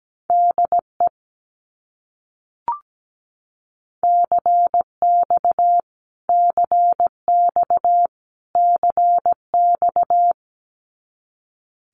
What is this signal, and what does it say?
Morse code, heard as a keyed tone, about 17 wpm, 700 Hz. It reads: DE CX CX CX
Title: Abbreviations Repeated 3x - 17wpm